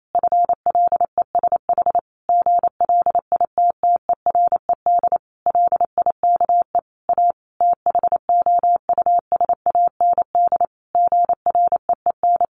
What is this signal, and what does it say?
Morse code, heard as a keyed tone, about 28 wpm, 700 Hz. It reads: FLESH GLITTERED LIKE A THOUSAND GREEN
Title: A Sound Of Thunder 28wpm